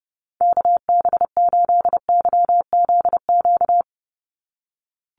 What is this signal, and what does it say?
Morse code, heard as a keyed tone, about 30 wpm, 700 Hz. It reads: KB8YZQ
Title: USA Call Signs 30wpm